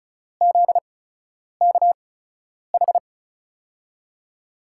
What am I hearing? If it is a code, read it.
Morse code, heard as a keyed tone, about 34 wpm, spaced as for 10 wpm, 700 Hz. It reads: ZKH
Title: USA Call Signs - 5 Characters - 8x Character Spacing - 35wpm